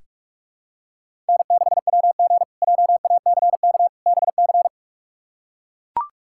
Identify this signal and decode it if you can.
Morse code, heard as a keyed tone, about 42 wpm, 700 Hz. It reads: N6WG JACK BC
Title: K1USN SST QSO Practice - 45wpm